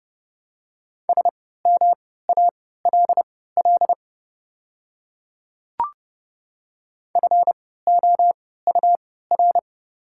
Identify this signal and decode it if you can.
Morse code, heard as a keyed tone, about 32 wpm, spaced as for 17 wpm, 690 Hz. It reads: SMALL FOUR
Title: 3x Character Spacing - Top 300 Words - 30wpm